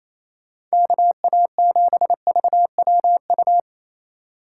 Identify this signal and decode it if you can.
Morse code, heard as a keyed tone, about 34 wpm, 700 Hz. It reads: KA7VWU
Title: Speed Racing - USA 2x3 Call Signs 28wpm